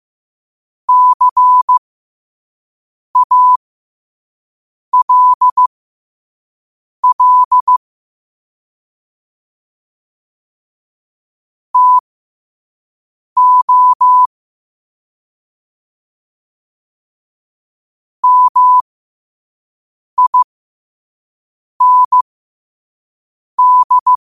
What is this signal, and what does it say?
Morse code, heard as a keyed tone, about 15 wpm, spaced as for 5 wpm, 1000 Hz. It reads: CALL TO MIND
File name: qotdfiles / 5wpm